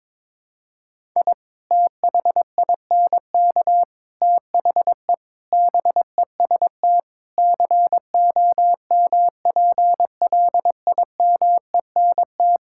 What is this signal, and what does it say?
Morse code, heard as a keyed tone, about 22 wpm, 700 Hz. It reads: I THINK THE BEST COMPLIMENT
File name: practice / 22wpm / Sentences with 8-Word Phrases - 22wpm